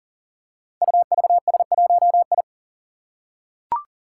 Morse code, heard as a keyed tone, about 38 wpm, 700 Hz. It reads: UVS1I
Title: Five Challenging Letter-Number Combinations 40wpm